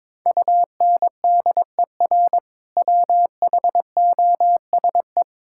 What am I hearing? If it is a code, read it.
Morse code, heard as a keyed tone, about 22 wpm, 700 Hz. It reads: UNDER WHOSE
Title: Sentences from Top 1600 Words 22wpm